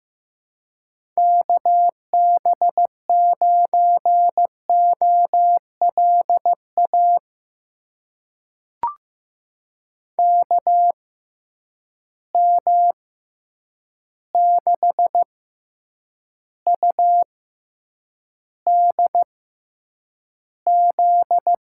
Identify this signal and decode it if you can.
Morse code, heard as a keyed tone, about 15 wpm, spaced as for 8 wpm, 700 Hz. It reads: KB9OLA KM6UDZ
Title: USA Call Signs - 6 Characters - 6x Character Spacing - 15wpm